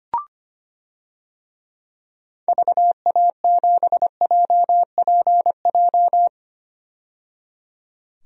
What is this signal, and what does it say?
Morse code, heard as a keyed tone, about 25 wpm, 700 Hz. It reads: VA7JPJ
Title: DXCC-100 Call Signs - 6 Characters - 25wpm